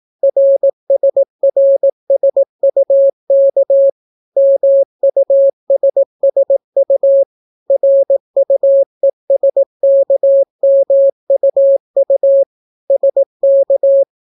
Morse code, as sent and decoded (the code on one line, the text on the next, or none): .-. ... .-. ... ..- -.- / -- ..- ... ... ..- / .-. ..- . ... -.- -- ..- ..- / ... -.-
RSRSUK MUSSU RUESKMUU SK